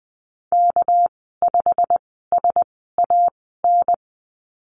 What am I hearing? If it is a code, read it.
Morse code, heard as a keyed tone, about 20 wpm, spaced as for 14 wpm, 700 Hz. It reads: K5SAN
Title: 200 Lesson 32g - Letter k - Call Signs 20wpm